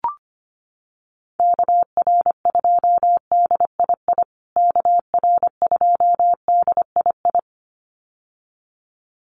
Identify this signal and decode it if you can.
Morse code, heard as a keyed tone, about 25 wpm, 700 Hz. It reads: KR2DII KR2DII
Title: USA Call Signs Repeated 2x 25wpm